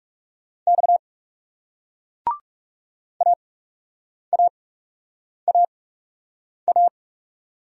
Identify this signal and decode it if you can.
Morse code, heard as a keyed tone, about 41 wpm, spaced as for 8 wpm, 700 Hz. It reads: X AAAA